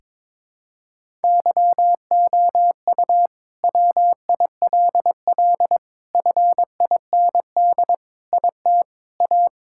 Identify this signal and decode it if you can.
Morse code, heard as a keyed tone, about 22 wpm, 700 Hz. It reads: YOU WILL FIND IT A
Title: Sentences from Top 1600 Words 22wpm